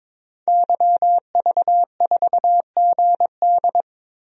Morse code, heard as a keyed tone, about 22 wpm, 700 Hz. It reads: YV4GD
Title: DXCC-100 Call Signs - 5 Characters - Repeated 2x - 22wpm